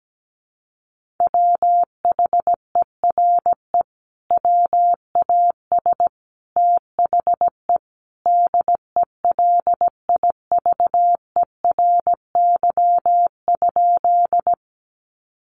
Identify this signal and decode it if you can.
Morse code, heard as a keyed tone, about 17 wpm, 700 Hz. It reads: WHERE WAS THE DELIVERY?
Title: Sentences from Top 1500 Words 17wpm